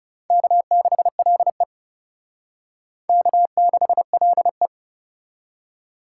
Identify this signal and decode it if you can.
Morse code, heard as a keyed tone, about 40 wpm, 700 Hz. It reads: K6LE K6LE